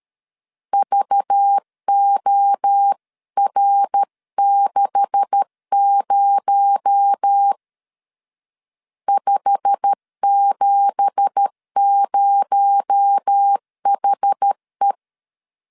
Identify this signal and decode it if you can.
Morse code, heard as a keyed tone, about 13 wpm, 800 Hz. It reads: VOR60 570HE
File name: CW / DET / 1 9 1500C S A 12wpm 3800